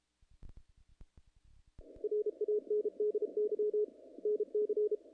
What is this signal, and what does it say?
Morse code, heard as a keyed tone, about 34 wpm, 440 Hz. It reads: RANDY NC
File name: contests / naqp-cw-k5tr-op-k5pi- -1800